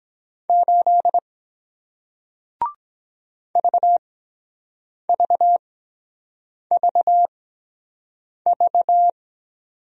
Morse code, heard as a keyed tone, about 23 wpm, spaced as for 7 wpm, 700 Hz. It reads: 8 VVVV